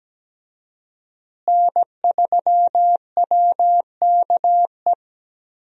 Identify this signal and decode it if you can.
Morse code, heard as a keyed tone, about 17 wpm, 700 Hz. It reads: N3WKE
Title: USA Call Signs - 5 Characters - Repeated 2x - 17wpm